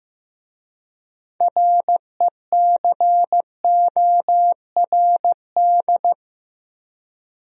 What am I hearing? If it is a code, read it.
Morse code, heard as a keyed tone, about 18 wpm, 700 Hz. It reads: RECORD
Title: Speed-Racing Top 500 Words 15wpm